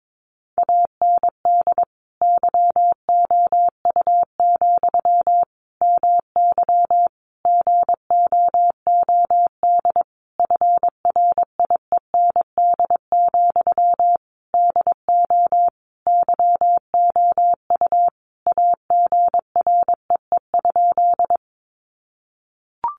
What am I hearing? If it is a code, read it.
Morse code, heard as a keyed tone, about 22 wpm, 700 Hz. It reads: AND YOU, MY GOOD FRIEND, DO YOU AGREE?
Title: Sentences from Top 1100 Words 22wpm